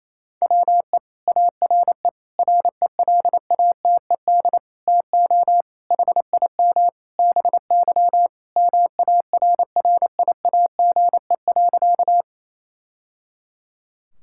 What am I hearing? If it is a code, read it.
Morse code, heard as a keyed tone, about 28 wpm, 700 Hz. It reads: WE ARE RELATED TO HIM BY MARRIAGE.
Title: Sentences from Top 1000 Words - Review Part 5 - 28wpm